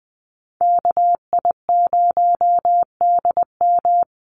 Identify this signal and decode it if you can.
Morse code, heard as a keyed tone, about 20 wpm, 700 Hz. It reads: KI0DM
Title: USA Call Signs Repeated 3x 20wpm